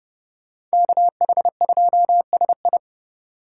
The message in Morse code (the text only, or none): KH2SI